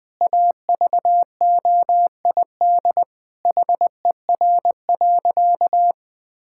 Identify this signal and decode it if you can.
Morse code, heard as a keyed tone, about 20 wpm, 700 Hz. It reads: AVOID HER.
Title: Sentences from Top 1000 Words - Review Part 6 - 20wpm